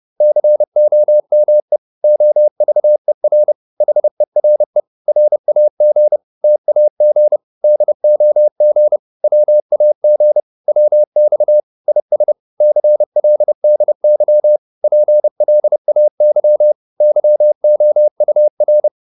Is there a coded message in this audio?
Yes: Morse code from 0.2 to 18.9 s, heard as a keyed tone, about 30 wpm, 600 Hz. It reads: COME OVER HERE RAG TAG DOG WAG WX IS CLDY PLAY YOUR